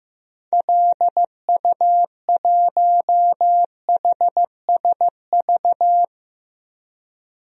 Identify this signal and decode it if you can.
Morse code, heard as a keyed tone, about 15 wpm, 700 Hz. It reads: LU1HSV
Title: DXCC-100 Call Signs - 6 Characters - 15wpm